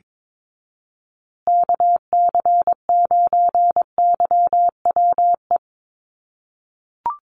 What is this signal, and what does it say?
Morse code, heard as a keyed tone, about 22 wpm, 700 Hz. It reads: KC9YWE
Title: USA Call Signs Repeated 3x 22wpm